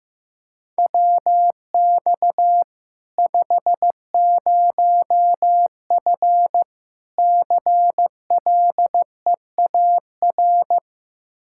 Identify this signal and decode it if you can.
Morse code, heard as a keyed tone, about 16 wpm, 700 Hz. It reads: WX 50F CLEAR
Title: Practice QSO - Exchange 2 - Info - 15wpm